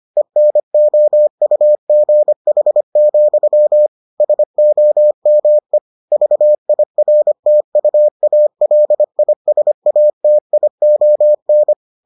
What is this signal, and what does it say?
Morse code, heard as a keyed tone, about 25 wpm, 600 Hz. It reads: ENOUGH, SOME VIRTUALISATION